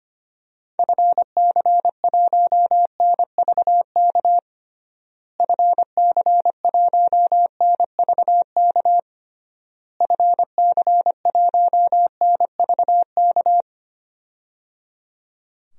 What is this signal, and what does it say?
Morse code, heard as a keyed tone, about 25 wpm, 700 Hz. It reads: FC1NVK FC1NVK FC1NVK
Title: DXCC-100 Call Signs - 6 Characters - Repeated 3x - 25wpm